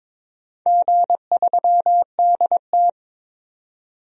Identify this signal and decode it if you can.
Morse code, heard as a keyed tone, about 23 wpm, 700 Hz. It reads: G3DT
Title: DXCC-100 Call Signs - 4 Characters - 22wpm